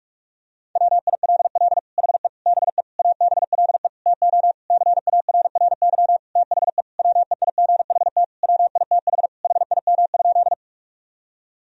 Morse code, heard as a keyed tone, about 41 wpm, 700 Hz. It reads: WILL HE BE ABLE TO CARRY THE WEIGHT WITH HIM?
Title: Sentences from Top 1000 Words - Review Part 3 - 45wpm